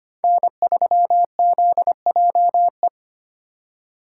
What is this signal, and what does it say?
Morse code, heard as a keyed tone, about 25 wpm, 700 Hz. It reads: N3ZJE
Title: USA Call Signs - 8x Character Spacing - 25wpm